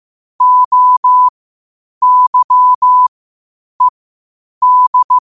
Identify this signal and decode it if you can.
Morse code, heard as a keyed tone, about 15 wpm, spaced as for 8 wpm, 1000 Hz. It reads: OYED
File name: qotdfiles / 8wpm